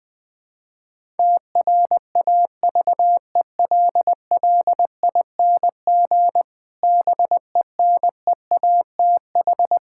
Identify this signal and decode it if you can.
Morse code, heard as a keyed tone, about 19 wpm, 700 Hz. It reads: TRAVELLING BENEATH
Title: Pangram Challenge v2 - 20wpm